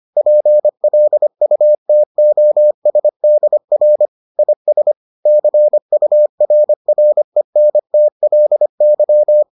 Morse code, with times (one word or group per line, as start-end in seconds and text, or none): 0.2-4.1 PLUTOSDR
4.4-4.9 IS
5.3-9.4 CURRENTLY